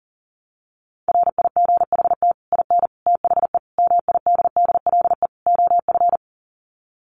Morse code, heard as a keyed tone, about 40 wpm, 700 Hz. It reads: RIGHT IN THE MIDDLE OF
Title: Sets of 5 Words 40wpm